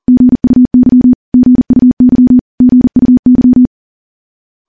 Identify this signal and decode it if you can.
Morse code, heard as a keyed tone, about 40 wpm, 250 Hz. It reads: GUY GUY GUY
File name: morse / morse-training-40WPM